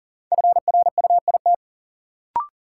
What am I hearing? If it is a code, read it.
Morse code, heard as a keyed tone, about 40 wpm, 700 Hz. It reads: FRUIT
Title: Sentences from Top 1300 Words - New Word Drill Alt - 40wpm